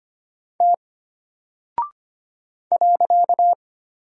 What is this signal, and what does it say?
Morse code, heard as a keyed tone, about 25 wpm, 700 Hz. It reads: T .
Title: Single Character (Letter:Number:Punctuation:Prosign) Alt 25wpm